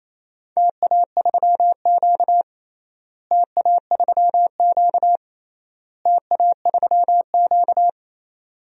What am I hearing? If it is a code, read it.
Morse code, heard as a keyed tone, about 28 wpm, 700 Hz. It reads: TA3Q TA3Q TA3Q
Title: DXCC-100 Call Signs - 4 Characters - Repeated 3x - 28wpm